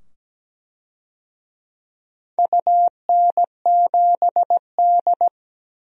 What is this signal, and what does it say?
Morse code, heard as a keyed tone, about 17 wpm, 700 Hz. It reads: UN7D
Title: DXCC-100 Call Signs - 4 Characters - 17wpm